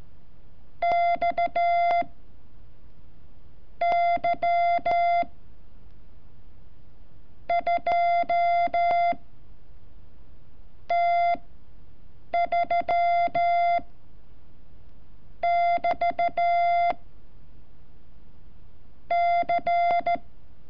Morse code, heard as a keyed tone, about 14 wpm, 690 Hz. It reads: X Y 2 T 3 <BT> C
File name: curso / ejercicio 29